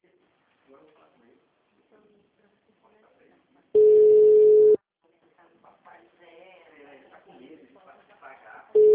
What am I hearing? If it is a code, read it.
Ringback tone (Europe, 425 Hz), with speech.
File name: out- -2005- - - .45918